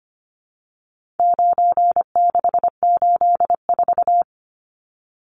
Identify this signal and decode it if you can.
Morse code, heard as a keyed tone, about 25 wpm, 700 Hz. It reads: 9684
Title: Numbers - 4-Digit 25wpm